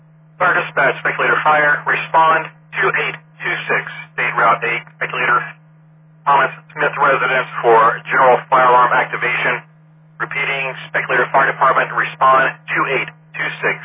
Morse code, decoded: none

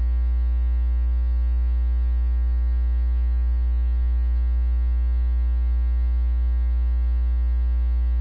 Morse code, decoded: none

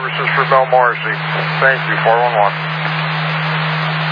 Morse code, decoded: none

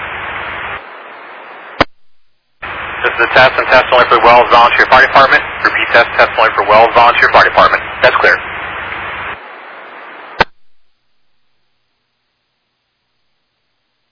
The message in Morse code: none

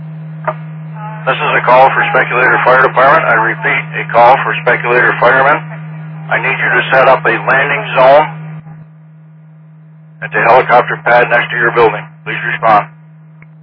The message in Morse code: none

